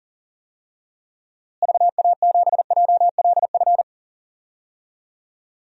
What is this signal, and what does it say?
Morse code, heard as a keyed tone, about 40 wpm, 700 Hz. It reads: VA7JLF
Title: DXCC-100 Call Signs - 6 Characters - Repeated 2x - 40wpm